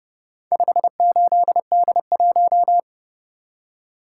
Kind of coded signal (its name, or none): Morse code